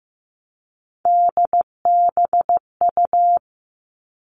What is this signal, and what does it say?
Morse code, heard as a keyed tone, about 15 wpm, 700 Hz. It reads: DBU
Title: All Cut Numbers - 3-Digit 15wpm